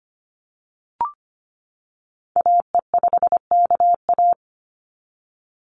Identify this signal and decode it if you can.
Morse code, heard as a keyed tone, about 25 wpm, 700 Hz. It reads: AE5KA
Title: NA SOTA Activator Callsigns 25wpm